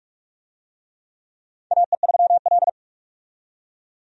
Morse code, heard as a keyed tone, about 46 wpm, 700 Hz. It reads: AE3L
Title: USA Call Signs - 4 Characters - 45wpm